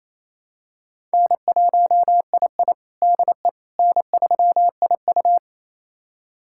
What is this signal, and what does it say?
Morse code, heard as a keyed tone, about 28 wpm, 700 Hz. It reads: N1II DE N3IU
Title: USA Callsign DE Callsign - 4 Characters - 28wpm